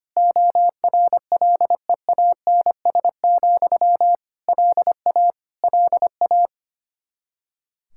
Morse code, heard as a keyed tone, about 25 wpm, 700 Hz. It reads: ORLEANS, LA LA
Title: Practice QSO - Exchange 1 - Beginning - 25wpm